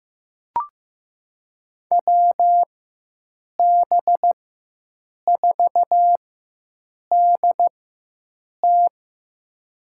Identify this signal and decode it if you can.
Morse code, heard as a keyed tone, about 15 wpm, spaced as for 7 wpm, 700 Hz. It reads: WB4DT
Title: USA Call Signs - 5 Characters - 4x Character Spacing - 15wpm